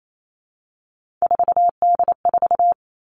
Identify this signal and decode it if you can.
Morse code, heard as a keyed tone, about 28 wpm, 700 Hz. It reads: 4D4
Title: All Cut Numbers - 3-Digit 28wpm